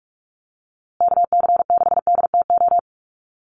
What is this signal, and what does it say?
Morse code, heard as a keyed tone, about 44 wpm, 700 Hz. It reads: KC6DTO
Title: USA Call Signs Repeated 3x 45wpm